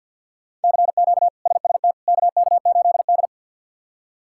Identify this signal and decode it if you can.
Morse code, heard as a keyed tone, about 51 wpm, 700 Hz. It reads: CQ SST KK8D
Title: K1USN SST QSO Practice - 50wpm